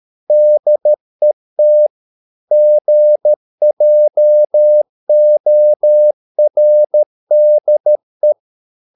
Morse code, heard as a keyed tone, about 13 wpm, 600 Hz. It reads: DET GJORDE
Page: Ferie 13 wpm